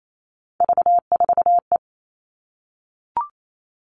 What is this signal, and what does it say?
Morse code, heard as a keyed tone, about 28 wpm, 700 Hz. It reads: V4E
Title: Three Challenging Letter-Number Combinations 28wpm